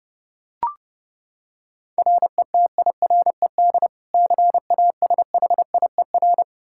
Morse code, heard as a keyed tone, about 30 wpm, 700 Hz. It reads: RETIRED CASHIER
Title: Practice QSO - Exchange 3 - More Info - 30wpm